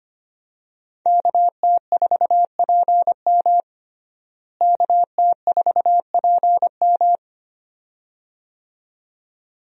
Morse code, heard as a keyed tone, about 25 wpm, 700 Hz. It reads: KT4PM KT4PM
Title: USA Call Signs - 5 Characters - Repeated 2x - 25wpm